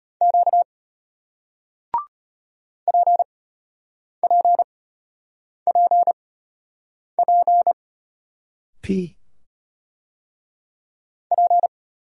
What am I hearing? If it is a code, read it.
Morse code, heard as a keyed tone, about 33 wpm, spaced as for 8 wpm, 700 Hz. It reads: Q PPPP P